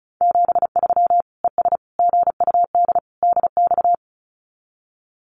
Morse code, heard as a keyed tone, about 34 wpm, 700 Hz. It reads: 73 ES GUD DX
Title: Common QSO Phrases 35wpm